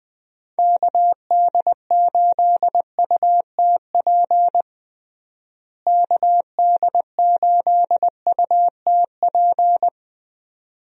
Morse code, heard as a keyed tone, about 20 wpm, 700 Hz. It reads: KD8UTP KD8UTP
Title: USA Call Signs - 6 Characters - Repeated 2x - 20wpm